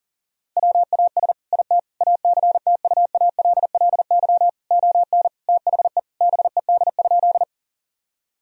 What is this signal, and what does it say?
Morse code, heard as a keyed tone, about 40 wpm, 700 Hz. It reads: WAS IT ACTUALLY ON THE BED?
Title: Sentences from Top 1000 Words - Review Part 4 - 40wpm